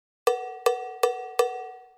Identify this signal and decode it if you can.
There is no code.